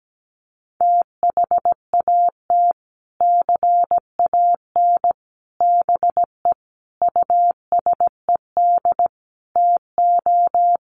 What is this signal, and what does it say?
Morse code, heard as a keyed tone, about 17 wpm, 700 Hz. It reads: THAT CAN BE USED TO
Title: Sets of 5 Words Encore 17wpm